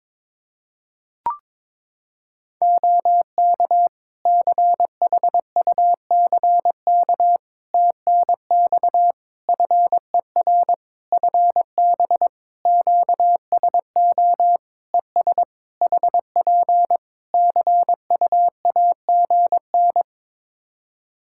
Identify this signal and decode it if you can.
Morse code, heard as a keyed tone, about 22 wpm, 700 Hz. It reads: OK CHUCK TNX FER FB QSO ES HP CUAGN
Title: Practice QSO - Exchange 4 - Ending - 22wpm